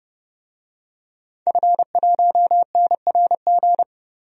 Morse code, heard as a keyed tone, about 30 wpm, 700 Hz. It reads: F1NRG
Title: DXCC-100 Call Signs - 5 Characters - Repeated 3x - 30wpm